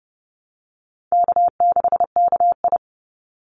Morse code, heard as a keyed tone, about 30 wpm, 700 Hz. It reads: K6KI
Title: Call Signs - 30wpm